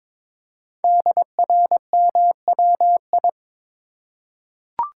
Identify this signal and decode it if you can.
Morse code, heard as a keyed tone, about 22 wpm, 700 Hz. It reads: DRMWI
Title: Code Groups - 5 Letters - 22wpm